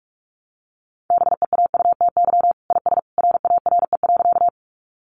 Morse code, heard as a keyed tone, about 44 wpm, 700 Hz. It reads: BEAUTY IS RARE.
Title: Sentences from Top 1100 Words 45wpm